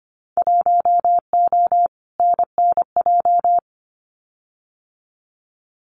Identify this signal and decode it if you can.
Morse code, heard as a keyed tone, about 25 wpm, 700 Hz. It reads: 1O NNJ
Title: Winter Field Day Exchanges 25wpm